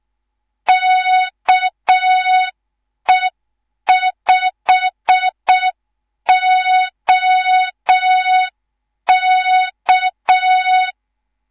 Morse code, heard as a keyed tone, about 6 wpm, 3700 Hz. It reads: KE5OK
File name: Morze-HA2MN / hang / 10GWZ HA2MN